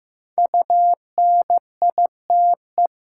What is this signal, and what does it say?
Morse code, heard as a keyed tone, about 15 wpm, 700 Hz. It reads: UNITE
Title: Sentences from Top 1000 Words - Review Part 5 - New Word Drill - 15wpm